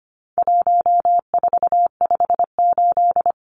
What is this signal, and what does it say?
Morse code, heard as a keyed tone, about 25 wpm, 700 Hz. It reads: 1458
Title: Numbers - 4-Digit 25wpm